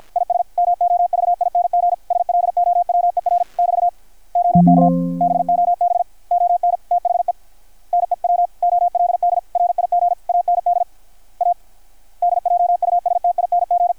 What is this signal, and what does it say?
Morse code, heard as a keyed tone, about 53 wpm, 720 Hz. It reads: IS MOVING SLOWER <BT> OK BOB ON THE NEW OLD RIG AND ADJUSTING
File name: archived / CRSnet